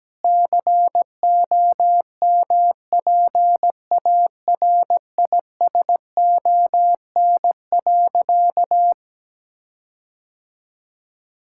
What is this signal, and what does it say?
Morse code, heard as a keyed tone, about 17 wpm, 700 Hz. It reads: COMPARISON.
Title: Sentences from Top 1600 Words 17wpm